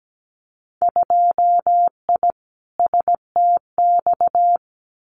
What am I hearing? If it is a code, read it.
Morse code, heard as a keyed tone, about 17 wpm, 690 Hz. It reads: 2I STX
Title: Winter Field Day Exchanges 17wpm